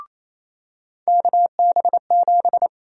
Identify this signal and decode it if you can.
Morse code, heard as a keyed tone, about 28 wpm, 700 Hz. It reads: KB7
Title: USA Call Sign Prefixes - 28wpm